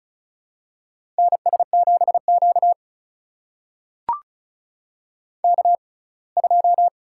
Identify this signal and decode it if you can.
Morse code, heard as a keyed tone, about 36 wpm, spaced as for 24 wpm, 700 Hz. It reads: NS7Q K2
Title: USA Call Signs - 4 Characters - 6x Character Spacing - 35wpm